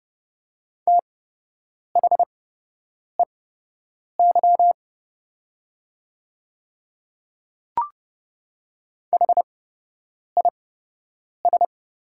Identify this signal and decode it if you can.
Morse code, heard as a keyed tone, about 29 wpm, spaced as for 8 wpm, 700 Hz. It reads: THEY HIS
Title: 8x Character Spacing - Top 100 Words - 30wpm